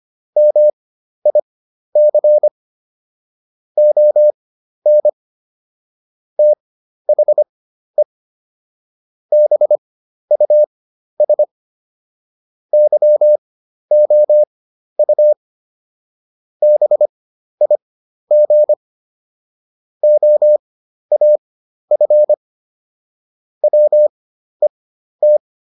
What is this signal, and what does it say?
Morse code, heard as a keyed tone, about 25 wpm, spaced as for 12 wpm, 600 Hz. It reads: MIC ON THE BUS YOU BIG OAF WET